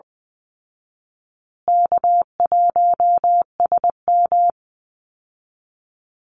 Morse code, heard as a keyed tone, about 20 wpm, 700 Hz. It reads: K1SM
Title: Easy Call Signs Repeated 3x - 20wpm